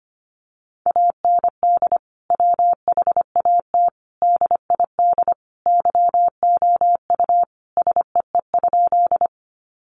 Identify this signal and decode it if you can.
Morse code, heard as a keyed tone, about 25 wpm, 700 Hz. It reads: AND WHAT DID YOU SEE?
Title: Sentences from Top 100 Words 25wpm